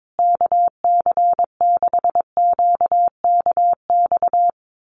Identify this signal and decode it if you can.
Morse code, heard as a keyed tone, about 22 wpm, 700 Hz. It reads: KC6QKX
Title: USA Call Signs Repeated 3x 22wpm